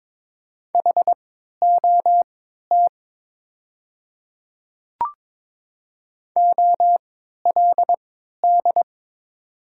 Morse code, heard as a keyed tone, about 22 wpm, spaced as for 13 wpm, 700 Hz. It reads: HOT OLD
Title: 3x Character Spacing - Top 300 Words - 22wpm